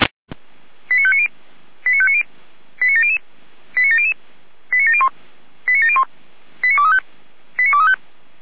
Five-tone selective call (ZVEI), heard as quick runs of five tones, tones of about 70 ms. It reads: 89590 89590 89700 89700 89701 89701 89235 89235